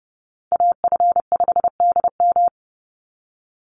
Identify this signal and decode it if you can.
Morse code, heard as a keyed tone, about 31 wpm, 700 Hz. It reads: AF5DM
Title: Call Signs - 30wpm